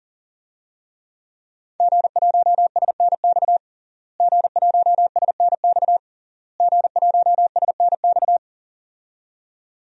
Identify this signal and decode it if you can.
Morse code, heard as a keyed tone, about 40 wpm, 700 Hz. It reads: G1SNX G1SNX G1SNX
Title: DXCC-100 Call Signs - 5 Characters - Repeated 3x - 40wpm